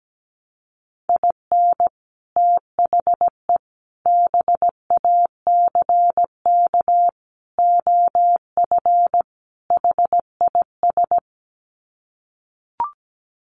Morse code, heard as a keyed tone, about 17 wpm, 700 Hz. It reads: IN THE BACK OF HIS
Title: Sets of 5 Words Encore2 17wpm